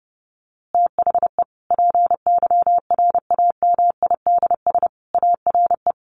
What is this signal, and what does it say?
Morse code, heard as a keyed tone, about 28 wpm, 700 Hz. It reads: THE PYRAMIDS ARE
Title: A Sound Of Thunder 30wpm